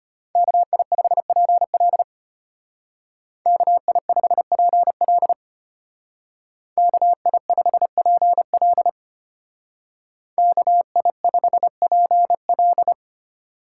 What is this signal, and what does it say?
Morse code, heard as a keyed tone, about 32 wpm, 700 Hz. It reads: KI5PL KI5PL KI5PL KI5PL